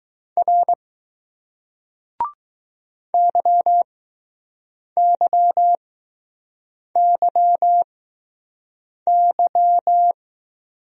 Morse code, heard as a keyed tone, about 20 wpm, 700 Hz. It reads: R Y Y Y Y